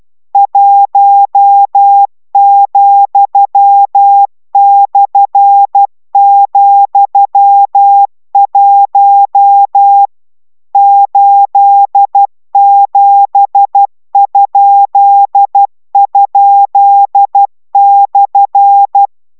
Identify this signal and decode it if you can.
Morse code, heard as a keyed tone, about 12 wpm, 800 Hz. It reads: 1,/,1 87??/